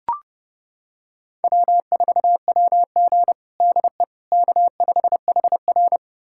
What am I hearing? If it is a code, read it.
Morse code, heard as a keyed tone, about 30 wpm, 700 Hz. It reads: W4WG DE K5HR
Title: USA Callsign DE Callsign - 4 Characters - 30wpm